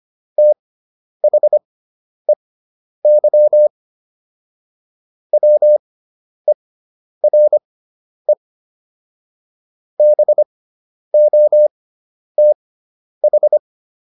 Morse code, heard as a keyed tone, about 25 wpm, spaced as for 10 wpm, 600 Hz. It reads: THEY WERE BOTH